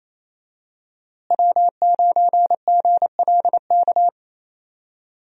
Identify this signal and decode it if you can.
Morse code, heard as a keyed tone, about 28 wpm, 700 Hz. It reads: W9GLK
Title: USA Call Signs - 5 Characters - Repeated 2x - 28wpm